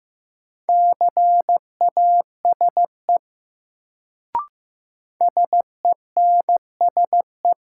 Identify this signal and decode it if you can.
Morse code, heard as a keyed tone, about 15 wpm, 700 Hz. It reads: CASE SENSE
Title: Sentences from Top 700 Words - New Word Drill - 15wpm